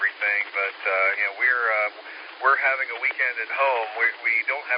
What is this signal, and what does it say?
Emergency Alert System (SAME) tones, heard as harsh buzzing bursts, with speech. It is heard from 1.1 to 4.5 s.